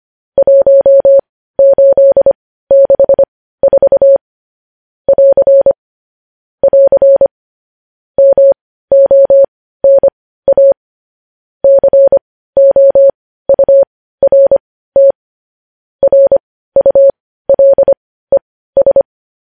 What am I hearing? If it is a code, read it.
Morse code, heard as a keyed tone, about 25 wpm, spaced as for 15 wpm, 550 Hz. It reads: 1864 <AR> <AR> MONA COURT RULES